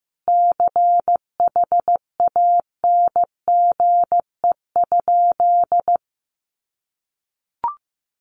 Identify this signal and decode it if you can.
Morse code, heard as a keyed tone, about 15 wpm, 700 Hz. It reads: CHANGE?
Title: Sentences from Top 1500 Words 15wpm